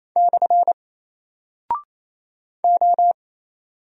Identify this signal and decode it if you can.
Morse code, heard as a keyed tone, about 27 wpm, 700 Hz. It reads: / O
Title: Single Character (Letter:Number:Punctuation:Prosign) 28wpm